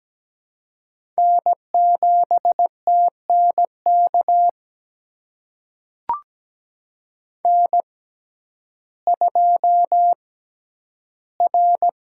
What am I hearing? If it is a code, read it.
Morse code, heard as a keyed tone, about 17 wpm, spaced as for 10 wpm, 700 Hz. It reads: N7TNK N2R
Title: USA Call Signs - 5 Characters - 6x Character Spacing - 17wpm